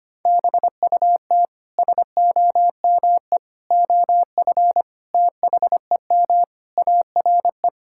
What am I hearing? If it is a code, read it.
Morse code, heard as a keyed tone, about 25 wpm, 700 Hz. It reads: BUT SOME OF THEM ARE
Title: Sentences from Top 1600 Words 25wpm